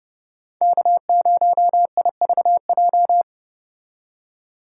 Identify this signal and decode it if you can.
Morse code, heard as a keyed tone, about 31 wpm, 700 Hz. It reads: K0IVJ